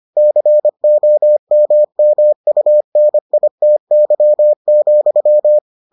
Morse code, heard as a keyed tone, about 25 wpm, 600 Hz. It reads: COMMUNITY,